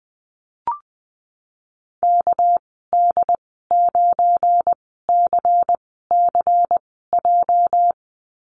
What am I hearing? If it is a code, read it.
Morse code, heard as a keyed tone, about 20 wpm, spaced as for 14 wpm, 700 Hz. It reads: KD9CCJ
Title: 207 Lesson 33g - Letter j - Call Signs 20wpm